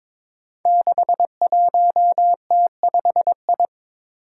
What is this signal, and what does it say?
Morse code, heard as a keyed tone, about 22 wpm, 700 Hz. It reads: 61T5I
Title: Five Challenging Letter-Number Combinations 22wpm